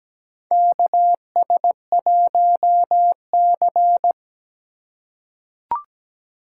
Morse code, heard as a keyed tone, about 17 wpm, 700 Hz. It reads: KS1C
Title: USA Call Signs - 3x Character Spacing - 17wpm